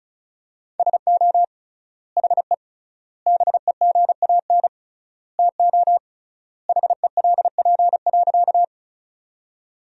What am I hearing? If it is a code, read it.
Morse code, heard as a keyed tone, about 36 wpm, 700 Hz. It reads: SO HE BEGAN TO HELP.
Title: 3x Word-Spacing - Sentences from Top 300 Words - 35wpm